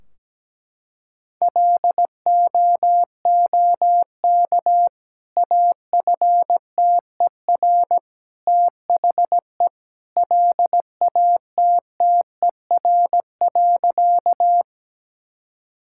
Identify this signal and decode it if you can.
Morse code, heard as a keyed tone, about 17 wpm, 700 Hz. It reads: LOOK AFTER THE LATTER.
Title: Sentences from Top 1600 Words 17wpm